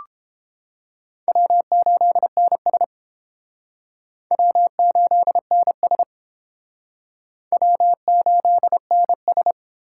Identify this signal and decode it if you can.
Morse code, heard as a keyed tone, about 29 wpm, 700 Hz. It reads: W8NS W8NS W8NS